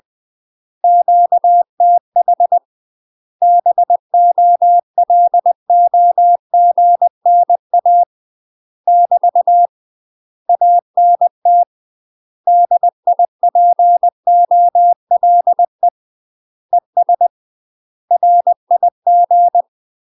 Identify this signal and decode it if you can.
Morse code, heard as a keyed tone, about 20 wpm, 700 Hz. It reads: QTH BOLOGNA <BT> ANT DIPOLE ES RIG
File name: UK MASTER QSOs TRAINING / 20WPM / UK MASTER QSOs 0008